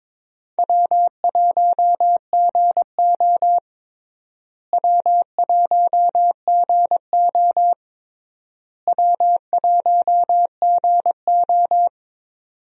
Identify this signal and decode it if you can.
Morse code, heard as a keyed tone, about 21 wpm, 700 Hz. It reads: W1GO W1GO W1GO
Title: USA Call Signs - 4 Characters - Repeated 3x - 22wpm